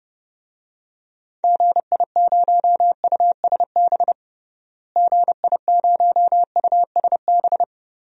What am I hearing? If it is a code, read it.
Morse code, heard as a keyed tone, about 30 wpm, 700 Hz. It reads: GI0USB GI0USB
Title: DXCC-100 Call Signs - 6 Characters - Repeated 3x - 30wpm